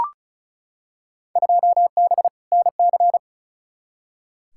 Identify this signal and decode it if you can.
Morse code, heard as a keyed tone, about 36 wpm, 690 Hz. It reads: 2B NC
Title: Field Day Exchanges Encore - 35wpm